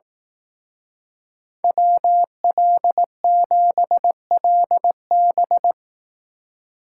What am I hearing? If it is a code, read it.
Morse code, heard as a keyed tone, about 18 wpm, 700 Hz. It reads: WL7LB
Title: Speed Racing - USA 2x2 Call Signs 15wpm